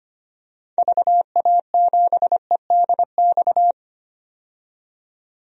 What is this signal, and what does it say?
Morse code, heard as a keyed tone, about 25 wpm, 700 Hz. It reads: VA7EDX
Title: DXCC-100 Call Signs - 6 Characters - 25wpm